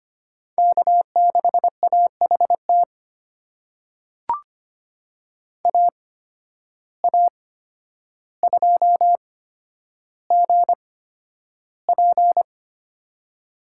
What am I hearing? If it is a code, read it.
Morse code, heard as a keyed tone, about 25 wpm, spaced as for 11 wpm, 700 Hz. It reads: K6AHT AA2GP
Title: USA Call Signs - 5 Characters - 8x Character Spacing - 25wpm